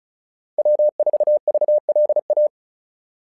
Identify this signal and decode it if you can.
Morse code, heard as a keyed tone, about 36 wpm, 600 Hz. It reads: W4VLA
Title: SOTA QSO Practice 35wpm